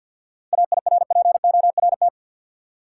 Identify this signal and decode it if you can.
Morse code, heard as a keyed tone, about 46 wpm, 700 Hz. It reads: AIRPORT
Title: Seven Letter Words - 50wpm